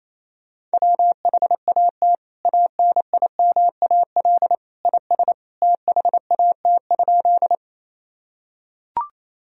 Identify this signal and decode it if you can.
Morse code, heard as a keyed tone, about 28 wpm, 700 Hz. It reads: WHAT ANIMAL IS THAT?
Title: Sentences from Top 1000 Words - Review Part 1 - 28wpm